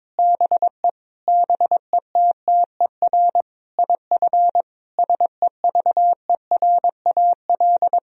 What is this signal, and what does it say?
Morse code, heard as a keyed tone, about 22 wpm, 700 Hz. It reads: BE BETTER IF SEVERAL
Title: Sentences from Top 1000 Words - Review Part 2 - 22wpm